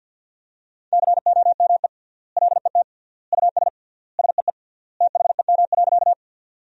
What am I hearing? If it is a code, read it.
Morse code, heard as a keyed tone, about 50 wpm, 700 Hz. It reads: COME LET US SEE THEM.
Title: 3x Word-Spacing - Sentences from Top 300 Words - 50wpm